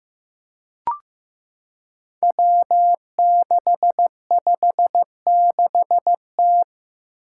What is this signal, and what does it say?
Morse code, heard as a keyed tone, about 15 wpm, 700 Hz. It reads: W656T
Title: Five Challenging Letter-Number Combinations 15wpm